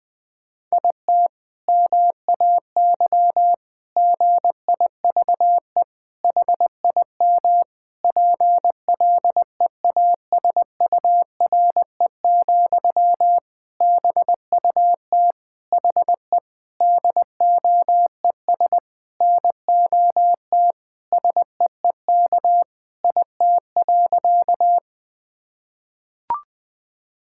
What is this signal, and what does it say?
Morse code, heard as a keyed tone, about 20 wpm, 700 Hz. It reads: IT MAY GIVE HIM PLEASURE, BUT HE DOES NOT SEEK IT.
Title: Sentences from Top 1600 Words 20wpm